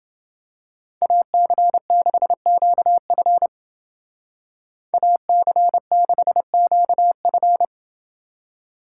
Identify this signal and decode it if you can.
Morse code, heard as a keyed tone, about 28 wpm, 700 Hz. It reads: AC6QF AC6QF